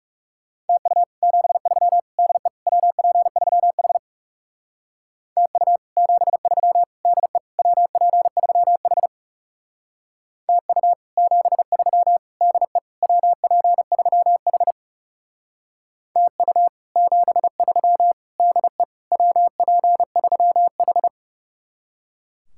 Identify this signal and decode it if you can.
Morse code, heard as a keyed tone, about 39 wpm, 700 Hz. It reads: TU 73 DE WP3H TU 73 DE WP3H TU 73 DE WP3H TU 73 DE WP3H